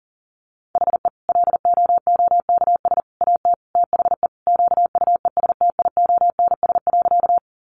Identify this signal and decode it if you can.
Morse code, heard as a keyed tone, about 40 wpm, 700 Hz. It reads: HE LOOKS AT THE QUESTIONS.
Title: Additional Easy Phrases 40wpm